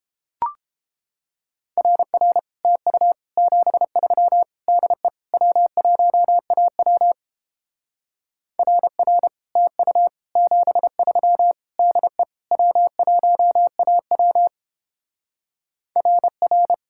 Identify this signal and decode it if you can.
Morse code, heard as a keyed tone, about 29 wpm, 700 Hz. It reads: RR TU 73 DE W1AW RR TU 73 DE W1AW RR